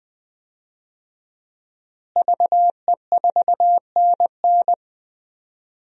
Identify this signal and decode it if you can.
Morse code, heard as a keyed tone, about 21 wpm, 700 Hz. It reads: VE4NN
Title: DXCC-100 Call Signs - 5 Characters - Repeated 2x - 20wpm